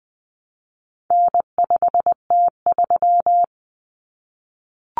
Morse code, heard as a keyed tone, about 20 wpm, 700 Hz. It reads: N5T3
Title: Common Cut Numbers - 4-Digit 20wpm